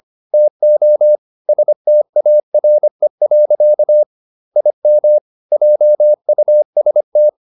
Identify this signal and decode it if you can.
Morse code, heard as a keyed tone, about 25 wpm, 600 Hz. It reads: TO STARE. IM JUST